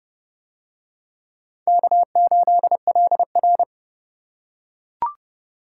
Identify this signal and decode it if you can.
Morse code, heard as a keyed tone, about 30 wpm, 700 Hz. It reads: K8LR